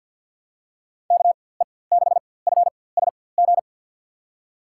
Morse code, heard as a keyed tone, about 52 wpm, spaced as for 23 wpm, 700 Hz. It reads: KE6FSG
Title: USA Call Signs - 4x Character Spacing - 50wpm